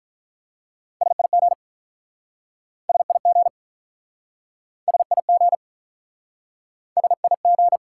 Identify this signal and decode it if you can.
Morse code, heard as a keyed tone, about 48 wpm, 700 Hz. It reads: SIG SIG SIG SIG